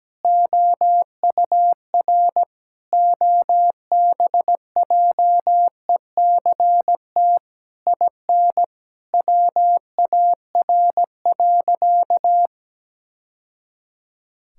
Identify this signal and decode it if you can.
Morse code, heard as a keyed tone, about 17 wpm, 700 Hz. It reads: OUR OBJECT IN WAR.
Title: Sentences from Top 1000 Words - Review Part 6 - 17wpm